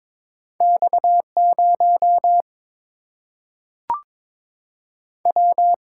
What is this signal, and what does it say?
Morse code, heard as a keyed tone, about 22 wpm, 700 Hz. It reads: X 0 W
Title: Single-Letter Single-Number 22wpm